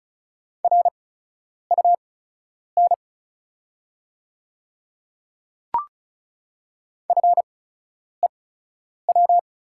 Morse code, heard as a keyed tone, about 36 wpm, spaced as for 10 wpm, 700 Hz. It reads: RUN FEW
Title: 8x Character Spacing - Top 300 Words - 35wpm